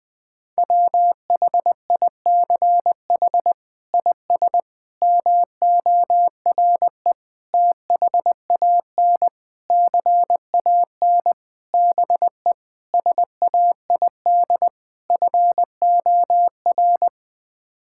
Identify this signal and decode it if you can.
Morse code, heard as a keyed tone, about 20 wpm, 700 Hz. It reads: WHICH IS MORE THAN CAN BE SAID FOR
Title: Sets of 8 Words - 20wpm